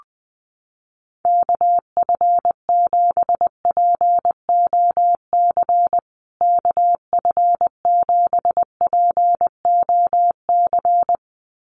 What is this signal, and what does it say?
Morse code, heard as a keyed tone, about 20 wpm, 700 Hz. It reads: KF7POC KF7POC
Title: USA Call Signs Repeated 2x 20wpm